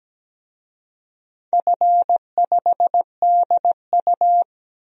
Morse code, heard as a keyed tone, about 17 wpm, 700 Hz. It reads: F5DU
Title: DXCC-100 Call Signs - 4 Characters - 17wpm